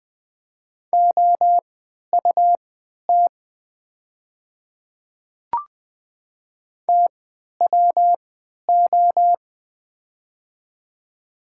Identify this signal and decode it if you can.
Morse code, heard as a keyed tone, about 20 wpm, spaced as for 11 wpm, 700 Hz. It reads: OUT TWO
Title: 3x Character Spacing - Top 100 Words - 20wpm